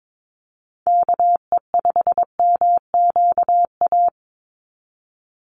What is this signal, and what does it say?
Morse code, heard as a keyed tone, about 22 wpm, 700 Hz. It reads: KE5MQA
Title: USA Call Signs 22wpm